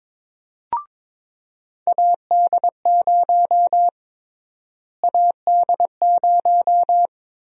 Morse code, heard as a keyed tone, about 22 wpm, 700 Hz. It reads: AD0 AD0
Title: USA Call Sign Prefixes - Repeated 2x - 22wpm